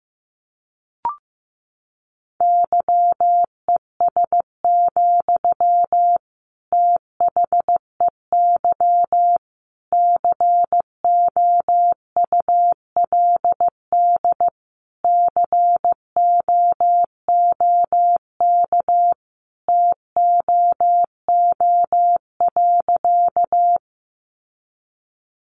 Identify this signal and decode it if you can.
Morse code, heard as a keyed tone, about 15 wpm, 700 Hz. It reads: YES, THEY COULD COOK TOO.
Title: Sentences from Top 700 Words 15wpm